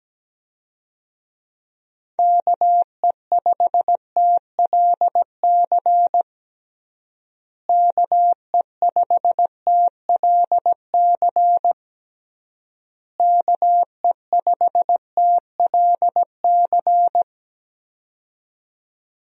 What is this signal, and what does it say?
Morse code, heard as a keyed tone, about 17 wpm, 700 Hz. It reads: KE5TLC KE5TLC KE5TLC
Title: USA Call Signs - 6 Characters - Repeated 3x - 17wpm